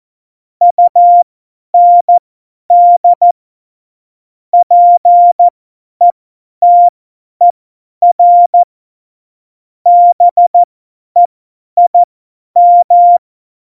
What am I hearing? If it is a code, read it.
Morse code, heard as a keyed tone, about 14 wpm, spaced as for 10 wpm, 700 Hz. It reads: UND PETER BEIM
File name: Morsekurs / Witze / Witz 27. Februar 2018, 50 F70